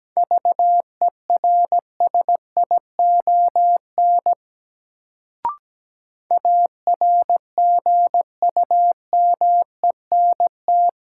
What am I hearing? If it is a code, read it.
Morse code, heard as a keyed tone, about 17 wpm, 700 Hz. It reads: VERSION ARGUMENT
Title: Sentences from Top 1400 Words - New Word Drill - 17wpm